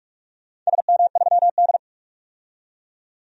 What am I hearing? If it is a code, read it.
Morse code, heard as a keyed tone, about 47 wpm, 700 Hz. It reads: SM3D
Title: DXCC-100 Call Signs - 4 Characters - 45wpm